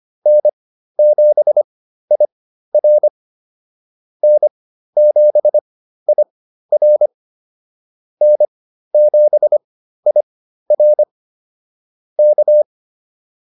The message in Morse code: N7IR N7IR N7IR K

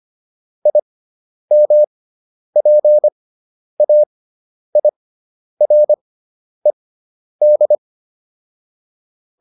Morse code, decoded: IMPAIRED

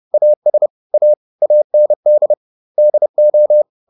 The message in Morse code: AS A AND DO

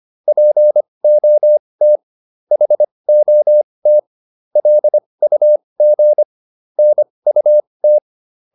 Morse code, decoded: POT HOT LUG NUT

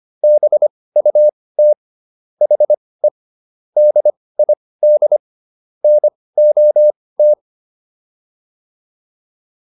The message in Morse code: BUT HE DID NOT

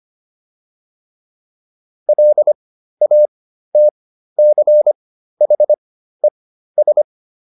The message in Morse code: LATCHES

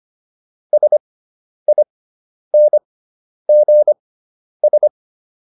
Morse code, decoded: SINGS

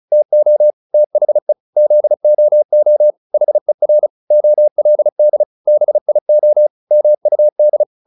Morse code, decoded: TO THE ZOO HER OLD BIO MUD

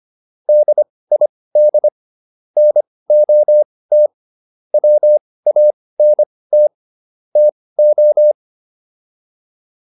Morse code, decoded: DID NOT WANT TO

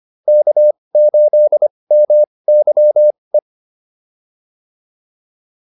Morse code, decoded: K8MYE

